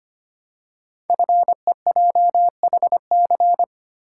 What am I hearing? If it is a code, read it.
Morse code, heard as a keyed tone, about 25 wpm, 700 Hz. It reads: FEJHC
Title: Code Groups - 5 Letters - 25wpm